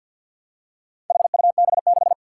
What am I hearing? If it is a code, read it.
Morse code, heard as a keyed tone, about 51 wpm, 700 Hz. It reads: HUB6
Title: Four Challenging Letter-Number Combinations 50wpm